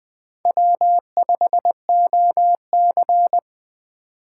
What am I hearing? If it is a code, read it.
Morse code, heard as a keyed tone, about 19 wpm, 700 Hz. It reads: W5OC